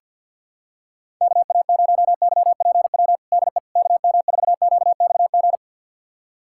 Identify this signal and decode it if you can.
Morse code, heard as a keyed tone, about 50 wpm, 690 Hz. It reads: KA0YPW DE KM4QXG
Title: USA Callsign DE Callsign - 6 Characters - 50wpm